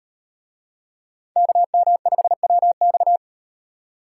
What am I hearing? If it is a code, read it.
Morse code, heard as a keyed tone, about 38 wpm, 700 Hz. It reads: KM5WX